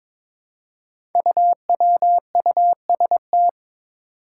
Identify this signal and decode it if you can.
Morse code, heard as a keyed tone, about 22 wpm, 700 Hz. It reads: UWUST
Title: Five Challenging Letter-Number Combinations 22wpm